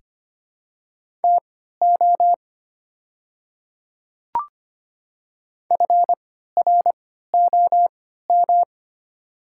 Morse code, heard as a keyed tone, about 25 wpm, spaced as for 14 wpm, 700 Hz. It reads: TO FROM
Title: 3x Character Spacing - Top 300 Words - 25wpm